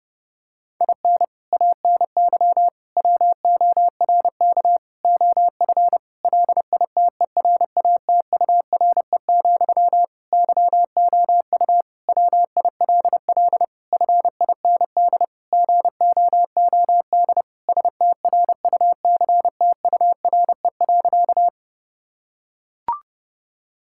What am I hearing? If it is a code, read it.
Morse code, heard as a keyed tone, about 30 wpm, 700 Hz. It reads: IN ANY WORK OF LITERATURE, YOU WILL FIND GOOD STRUCTURE.
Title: Sentences from Top 1000 Words - Review Part 6 - 30wpm